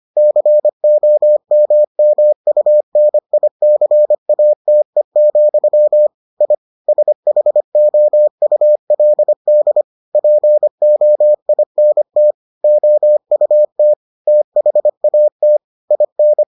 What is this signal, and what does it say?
Morse code, heard as a keyed tone, about 25 wpm, 600 Hz. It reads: COMMUNICATE, I SHOULD POINT OUT THAT IN